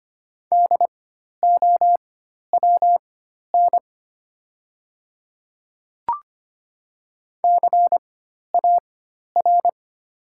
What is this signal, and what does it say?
Morse code, heard as a keyed tone, about 25 wpm, spaced as for 12 wpm, 700 Hz. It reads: DOWN CAR
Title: 4x Character Spacing - Top 300 Words - 25wpm